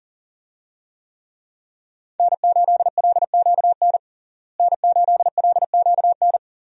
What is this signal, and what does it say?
Morse code, heard as a keyed tone, about 40 wpm, 700 Hz. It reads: N8LQN N8LQN
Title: USA Call Signs - 5 Characters - Repeated 2x - 40wpm